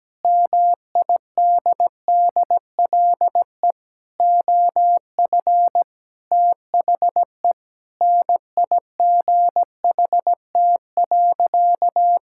Morse code, heard as a keyed tone, about 17 wpm, 700 Hz. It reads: MIDDLE OF THE NIGHT.
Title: Sentences from Top 1000 Words - Review Part 6 - 17wpm